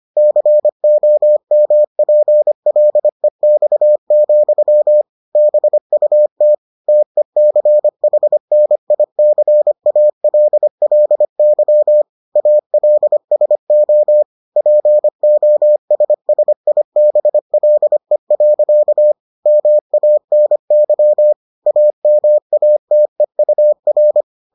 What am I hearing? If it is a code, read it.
Morse code, heard as a keyed tone, about 25 wpm, 600 Hz. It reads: COMPLEX, BUT TECHNICALLY ALSO POSSIBLE. MANY AMATEUR